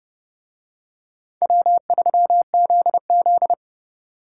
Morse code, heard as a keyed tone, about 29 wpm, 700 Hz. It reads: W3ZZ